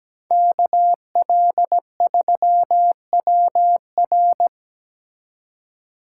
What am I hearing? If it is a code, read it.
Morse code, heard as a keyed tone, about 17 wpm, 700 Hz. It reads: KL3WR
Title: USA Call Signs - 5 Characters - 4x Character Spacing - 17wpm